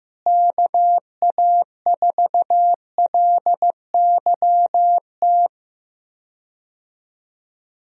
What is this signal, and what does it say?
Morse code, heard as a keyed tone, about 16 wpm, 700 Hz. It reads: KA4LYT
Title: USA Call Signs - 6 Characters - Repeated 3x - 15wpm